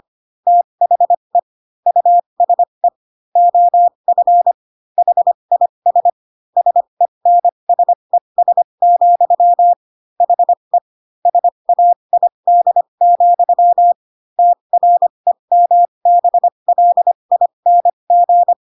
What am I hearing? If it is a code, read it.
Morse code, heard as a keyed tone, about 25 wpm, spaced as for 22 wpm, 700 Hz. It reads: THE USE OF HIS SENSES, HE SAID, TREMBLING